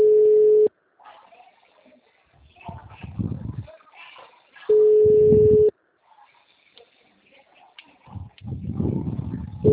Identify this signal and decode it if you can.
Ringback tone (Europe, 425 Hz).
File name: out- -2009- - - .599